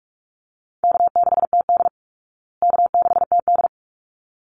Morse code, heard as a keyed tone, about 44 wpm, 700 Hz. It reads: K6TD K6TD
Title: Call Signs Repeated 3x - 45wpm